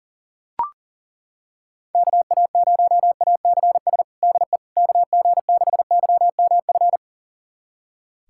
Morse code, heard as a keyed tone, about 40 wpm, 700 Hz. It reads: KA0ACS DE KG6YMF
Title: USA Callsign DE Callsign 40wpm